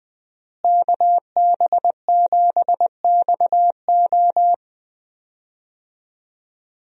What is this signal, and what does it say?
Morse code, heard as a keyed tone, about 20 wpm, 700 Hz. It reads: KB7XO
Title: USA Call Signs - 5 Characters - Repeated 3x - 20wpm